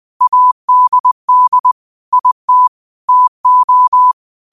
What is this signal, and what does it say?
Morse code, heard as a keyed tone, about 20 wpm, 1000 Hz. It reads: ADD IT TO
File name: qotdfiles / 20wpm